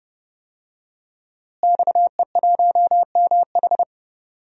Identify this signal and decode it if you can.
Morse code, heard as a keyed tone, about 30 wpm, 700 Hz. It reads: XE1MH
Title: DXCC-100 Call Signs - 5 Characters - Repeated 3x - 30wpm